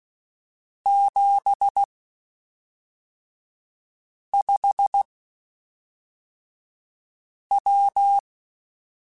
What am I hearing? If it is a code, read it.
Morse code, heard as a keyed tone, about 16 wpm, spaced as for 3 wpm, 800 Hz. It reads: 75W
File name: (spaced) / QSOs Spaced-mp3s / WA7QOX 1 @5 sp